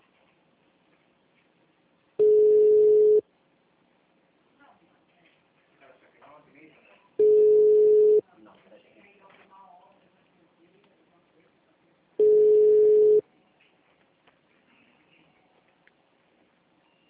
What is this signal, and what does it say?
Ringback tone (Europe, 425 Hz), with speech.
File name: out- -2007- - - .200